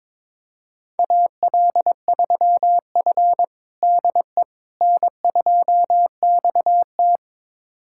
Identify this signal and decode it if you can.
Morse code, heard as a keyed tone, about 22 wpm, 700 Hz. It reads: AL3F DE N2XT
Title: USA Callsign DE Callsign - 4 Characters - 22wpm